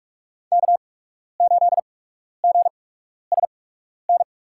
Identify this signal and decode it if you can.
Morse code, heard as a keyed tone, about 42 wpm, spaced as for 12 wpm, 700 Hz. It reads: K8GSN
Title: USA Call Signs - 8x Character Spacing - 45wpm